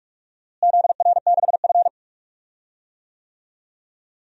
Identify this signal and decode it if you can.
Morse code, heard as a keyed tone, about 41 wpm, 700 Hz. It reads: ZR6F
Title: DXCC-100 Call Signs - 4 Characters - Repeated 3x - 45wpm